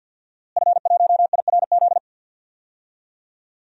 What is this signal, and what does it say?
Morse code, heard as a keyed tone, about 52 wpm, 700 Hz. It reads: F1IRZ
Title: DXCC-100 Call Signs - 5 Characters - Repeated 3x - 50wpm